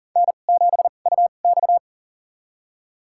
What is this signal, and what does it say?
Morse code, heard as a keyed tone, about 40 wpm, spaced as for 29 wpm, 700 Hz. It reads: N7UX
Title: USA Call Signs - 2x Character Spacing - 40wpm